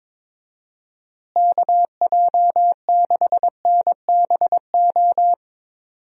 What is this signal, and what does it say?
Morse code, heard as a keyed tone, about 22 wpm, 700 Hz. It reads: KJ6NBO
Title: USA Call Signs - 6 Characters - Repeated 2x - 22wpm